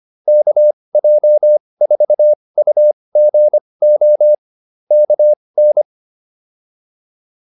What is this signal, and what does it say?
Morse code, heard as a keyed tone, about 25 wpm, spaced as for 20 wpm, 600 Hz. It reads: KJ4UGO KN
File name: QSO201a 20